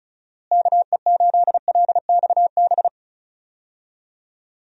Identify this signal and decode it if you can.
Morse code, heard as a keyed tone, about 34 wpm, 700 Hz. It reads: KE8LXB
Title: USA Call Signs - 6 Characters - Repeated 2x - 35wpm